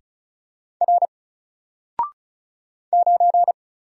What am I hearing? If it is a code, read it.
Morse code, heard as a keyed tone, about 34 wpm, 700 Hz. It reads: R 9
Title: Single Character (Letter:Number:Punctuation:Prosign) 35wpm